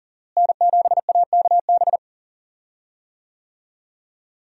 Morse code, heard as a keyed tone, about 40 wpm, 700 Hz. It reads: N7AKB
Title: USA Call Signs - 5 Characters - Repeated 2x - 40wpm